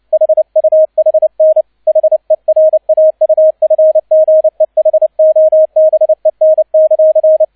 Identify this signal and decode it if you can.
Morse code, heard as a keyed tone, about 29 wpm, 630 Hz, with radio noise. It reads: HUHN HERAUFGEHOBEN;
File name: 002 Zweiter Streich 30wpm